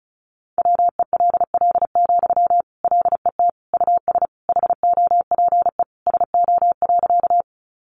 Morse code, heard as a keyed tone, about 34 wpm, 700 Hz. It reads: WELL, LET US HOPE SO.
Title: Sentences from Top 1000 Words 35wpm